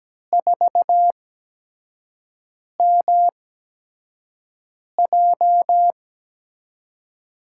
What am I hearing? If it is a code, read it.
Morse code, heard as a keyed tone, about 17 wpm, spaced as for 5 wpm, 700 Hz. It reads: 4MJ
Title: USA Call Signs - 5 Characters - 8x Character Spacing - 17wpm